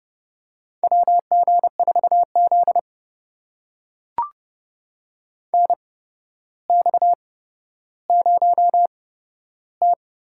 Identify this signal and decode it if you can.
Morse code, heard as a keyed tone, about 32 wpm, spaced as for 13 wpm, 700 Hz. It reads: WG4Z NX0T
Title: USA Call Signs - 4 Characters - 8x Character Spacing - 30wpm